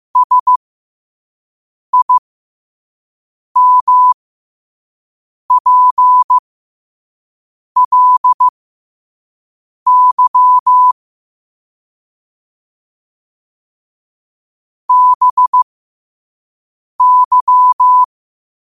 Morse code, heard as a keyed tone, about 15 wpm, spaced as for 5 wpm, 1000 Hz. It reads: SIMPLY BY
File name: 5wpm